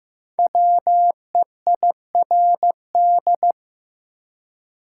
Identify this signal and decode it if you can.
Morse code, heard as a keyed tone, about 15 wpm, 700 Hz. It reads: WEIRD
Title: Five Letter Words - 15wpm